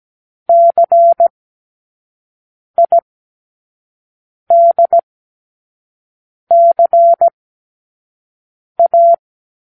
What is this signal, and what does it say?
Morse code, heard as a keyed tone, about 17 wpm, spaced as for 5 wpm, 680 Hz. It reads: CIDCA
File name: k7qo / Lesson15(A-J)